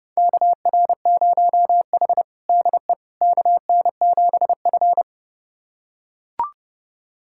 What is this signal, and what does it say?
Morse code, heard as a keyed tone, about 32 wpm, 700 Hz. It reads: KR0H DE KN7F
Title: USA Callsign DE Callsign - 4 Characters - 30wpm